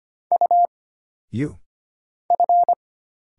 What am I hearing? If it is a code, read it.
Morse code, heard as a keyed tone, about 25 wpm, 700 Hz. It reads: U F
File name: Single Character (Letter:Number:Punctuation:Prosign) - Mind-Melt - 25wpm